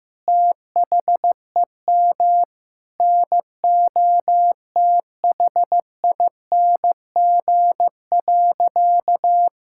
Morse code, heard as a keyed tone, about 15 wpm, 700 Hz. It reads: THEM NOTHING.
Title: Sentences from Top 1000 Words - Review Part 6 - 15wpm